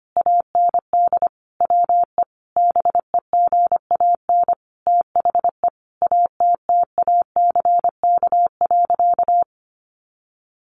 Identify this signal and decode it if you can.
Morse code, heard as a keyed tone, about 25 wpm, 700 Hz. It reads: AND WE BEGAN THE ATTACK.
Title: Sentences from Top 1500 Words 25wpm